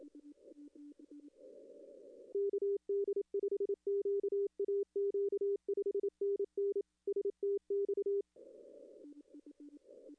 Morse code, heard as a keyed tone, about 28 wpm, 400 Hz. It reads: KD5QAQ 5NN STX
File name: contests / arrl-160-k5tr-op-w5tn -2300